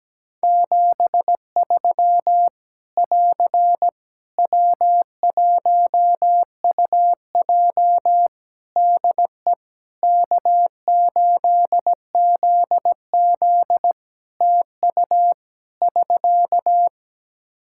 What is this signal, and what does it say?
Morse code, heard as a keyed tone, about 17 wpm, 700 Hz. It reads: 73 <AR> W1UJ DE K8ZZ TU <SK>
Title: Practice QSO - Exchange 4 - Ending - 17wpm